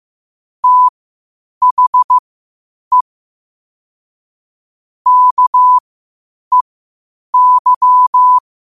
Morse code, heard as a keyed tone, about 15 wpm, spaced as for 8 wpm, 1000 Hz. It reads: THE KEY
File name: qotdfiles / 8wpm